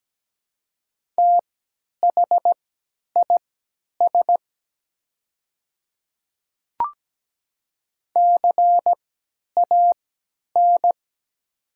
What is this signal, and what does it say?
Morse code, heard as a keyed tone, about 17 wpm, spaced as for 10 wpm, 700 Hz. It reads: THIS CAN
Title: 3x Character Spacing - Top 200 Words - 17wpm